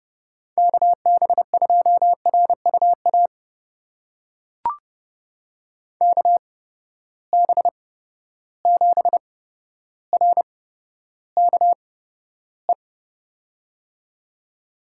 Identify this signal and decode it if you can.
Morse code, heard as a keyed tone, about 31 wpm, spaced as for 13 wpm, 700 Hz. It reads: KB2RUA KB7RKE
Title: USA Call Signs - 6 Characters - 8x Character Spacing - 30wpm